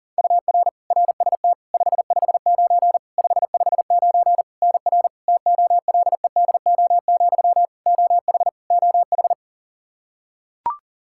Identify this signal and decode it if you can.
Morse code, heard as a keyed tone, about 40 wpm, 700 Hz. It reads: UR RST 559 559 NR TOLEDO, OH OH
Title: Practice QSO - Exchange 1 - Beginning - 40wpm